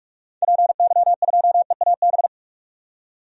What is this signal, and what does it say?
Morse code, heard as a keyed tone, about 44 wpm, 700 Hz. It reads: PY2EAB
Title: DXCC-100 Call Signs - 6 Characters - 45wpm